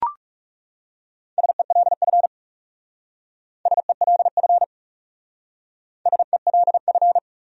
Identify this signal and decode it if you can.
Morse code, heard as a keyed tone, about 38 wpm, 700 Hz. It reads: SELF SELF SELF